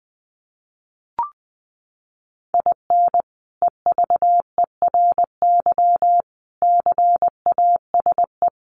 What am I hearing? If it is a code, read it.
Morse code, heard as a keyed tone, about 19 wpm, 700 Hz. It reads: IN EVERY CASE
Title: Sentences from Top 1200 Words 20wpm